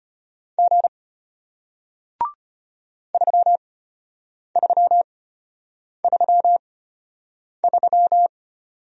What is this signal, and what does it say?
Morse code, heard as a keyed tone, about 34 wpm, spaced as for 8 wpm, 700 Hz. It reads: G 3333